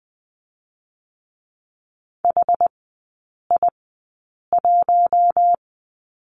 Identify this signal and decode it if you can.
Morse code, heard as a keyed tone, about 20 wpm, 700 Hz. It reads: H I 1
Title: 100 Lesson 17e - Letter u - Three Characters 20wpm